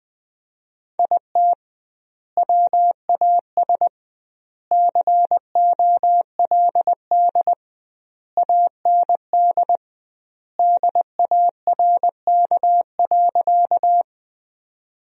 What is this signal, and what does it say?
Morse code, heard as a keyed tone, about 20 wpm, 700 Hz. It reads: IT WAS COLD AND DARK.
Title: 2x Word-Spacing - Sentences from Top 500 Words - 20wpm